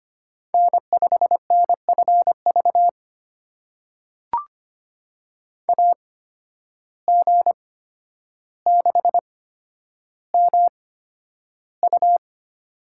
Morse code, heard as a keyed tone, about 25 wpm, spaced as for 11 wpm, 700 Hz. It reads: N5NFV AG6MU
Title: USA Call Signs - 5 Characters - 8x Character Spacing - 25wpm